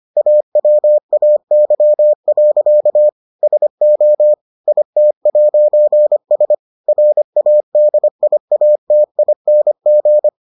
Morse code, heard as a keyed tone, about 25 wpm, 600 Hz. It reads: AWAY. SO IT'S RADIATING